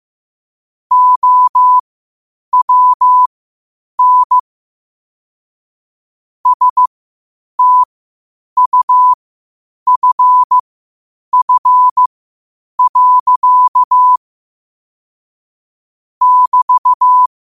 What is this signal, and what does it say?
Morse code, heard as a keyed tone, about 15 wpm, spaced as for 8 wpm, 1000 Hz. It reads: OWN STUFF. <BT>
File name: qotdfiles / 8wpm